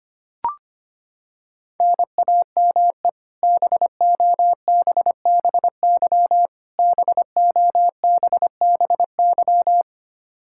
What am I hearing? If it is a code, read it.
Morse code, heard as a keyed tone, about 25 wpm, 700 Hz. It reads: NAME BOBBY BOBBY
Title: Practice QSO - Exchange 1 - Beginning - 25wpm